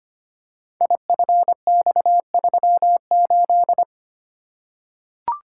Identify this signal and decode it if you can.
Morse code, heard as a keyed tone, about 25 wpm, 700 Hz. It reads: IFX38
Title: Code Groups - 5 Letter:Numbers - 25wpm